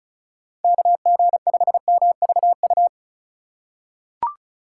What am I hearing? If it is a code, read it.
Morse code, heard as a keyed tone, about 36 wpm, 700 Hz. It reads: KG5MVU
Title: USA Call Signs - 6 Characters - 6x Character Spacing - 35wpm